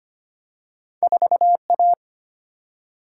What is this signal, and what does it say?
Morse code, heard as a keyed tone, about 25 wpm, 700 Hz. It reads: 4A
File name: Two Letter-Number Combinations - 25wpm